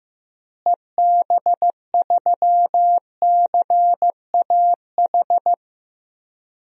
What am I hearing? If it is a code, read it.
Morse code, heard as a keyed tone, about 15 wpm, 700 Hz. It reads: EB3CAH
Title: DXCC-100 Call Signs - 6 Characters - Repeated 2x - 15wpm